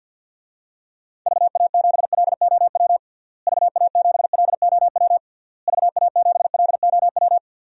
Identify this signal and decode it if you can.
Morse code, heard as a keyed tone, about 52 wpm, 700 Hz. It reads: VA7LOW VA7LOW VA7LOW
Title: DXCC-100 Call Signs - 6 Characters - Repeated 3x - 50wpm